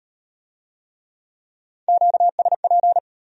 Morse code, heard as a keyed tone, about 32 wpm, 700 Hz. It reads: QSP